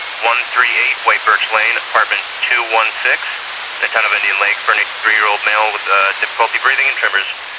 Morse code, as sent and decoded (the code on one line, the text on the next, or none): none